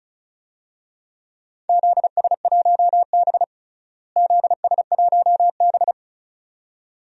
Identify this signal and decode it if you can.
Morse code, heard as a keyed tone, about 36 wpm, 700 Hz. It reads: ZS1B ZS1B
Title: DXCC-100 Call Signs - 4 Characters - Repeated 2x - 35wpm